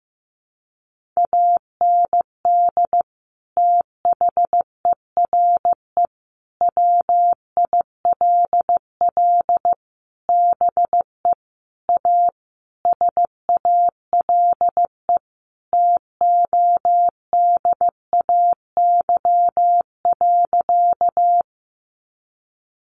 Morse code, heard as a keyed tone, about 15 wpm, 700 Hz. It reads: AND THERE WILL BE A SALE TODAY.
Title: Sentences from Top 1400 Words 15wpm